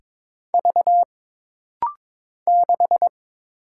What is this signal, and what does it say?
Morse code, heard as a keyed tone, about 22 wpm, 700 Hz. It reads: V 6
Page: This rapid-fire timing is geared to push your ICR (Instant Character Recognition) skills.